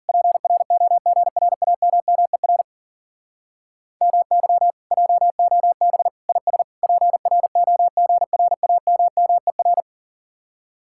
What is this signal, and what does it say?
Morse code, heard as a keyed tone, about 52 wpm, 700 Hz. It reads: PROGRAMMER MY JOB IS PROGRAMMER